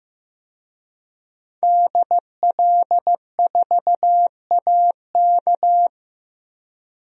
Morse code, heard as a keyed tone, about 15 wpm, 700 Hz. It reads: DL4AK
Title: DXCC-100 Call Signs - 5 Characters - Repeated 2x - 15wpm